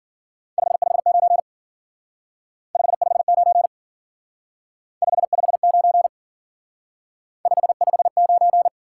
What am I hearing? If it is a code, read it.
Morse code, heard as a keyed tone, about 50 wpm, 700 Hz. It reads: 559 559 559 559